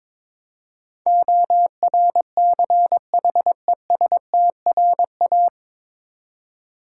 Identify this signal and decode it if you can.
Morse code, heard as a keyed tone, about 22 wpm, 700 Hz. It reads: ORCHESTRA
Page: Morse Code Ninja practice 22wpm